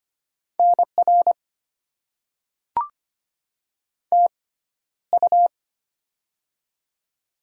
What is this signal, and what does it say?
Morse code, heard as a keyed tone, about 25 wpm, spaced as for 13 wpm, 700 Hz. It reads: NR TU
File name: Top 90 QSO Elements - 6x Character Spacing - 25wpm